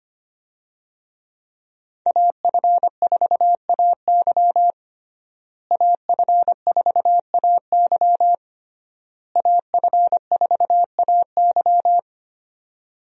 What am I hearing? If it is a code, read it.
Morse code, heard as a keyed tone, about 25 wpm, 700 Hz. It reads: AF4AY AF4AY AF4AY
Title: USA Call Signs - 5 Characters - Repeated 3x - 25wpm